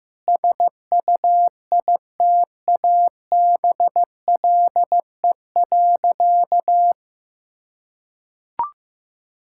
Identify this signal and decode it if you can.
Morse code, heard as a keyed tone, about 15 wpm, 700 Hz. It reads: SUITABLE.
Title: Sentences from Top 1600 Words 15wpm